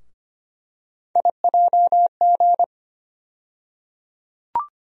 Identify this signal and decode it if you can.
Morse code, heard as a keyed tone, about 25 wpm, 700 Hz. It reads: IJG
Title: USA Call Sign Postfixes - 3 Letters 25wpm